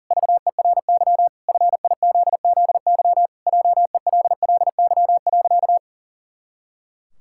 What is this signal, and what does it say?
Morse code, heard as a keyed tone, about 40 wpm, 700 Hz. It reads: VERY FIZZY JELLY.
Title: Pangram Challenge v2 - 40wpm